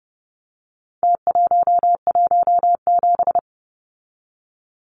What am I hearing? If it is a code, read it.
Morse code, heard as a keyed tone, about 30 wpm, 700 Hz. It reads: T117
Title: Numbers - Contest Style - 4-Digit 30wpm